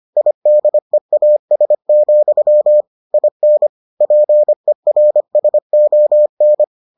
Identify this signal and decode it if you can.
Morse code, heard as a keyed tone, about 25 wpm, 600 Hz. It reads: IDEAS, IN PERSON